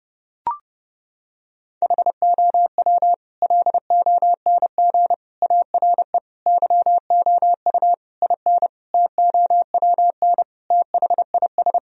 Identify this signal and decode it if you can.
Morse code, heard as a keyed tone, about 32 wpm, 700 Hz. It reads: HOW LONG ARE YOU IN TOWN THIS
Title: Sentences from Top 1000 Words - Review Part 2 - 30wpm